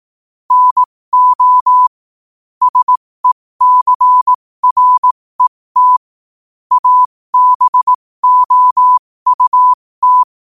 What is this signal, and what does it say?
Morse code, heard as a keyed tone, about 18 wpm, spaced as for 15 wpm, 1000 Hz. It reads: NO SECRET ABOUT
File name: qotdfiles / 15wpm